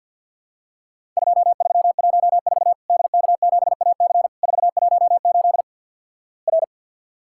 Morse code, heard as a keyed tone, about 53 wpm, 700 Hz. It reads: 2314 DK7AC 418 R
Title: Worked All Europe DX Contest - QTC Exchange - 50wpm